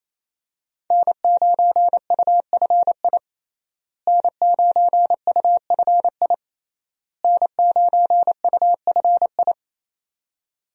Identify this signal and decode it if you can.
Morse code, heard as a keyed tone, about 28 wpm, 700 Hz. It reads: N9UFI N9UFI N9UFI
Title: USA Call Signs - 5 Characters - Repeated 3x - 28wpm